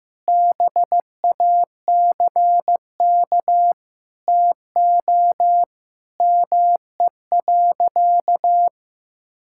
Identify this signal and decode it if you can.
Morse code, heard as a keyed tone, about 15 wpm, 700 Hz. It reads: BACK TO ME.
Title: Sentences from Top 1000 Words - Review Part 6 - 15wpm